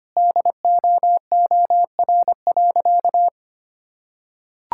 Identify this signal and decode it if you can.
Morse code, heard as a keyed tone, about 25 wpm, 700 Hz. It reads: DOOR.
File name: practice / 25wpm / Sentences with 2-Word Phrases - 25wpm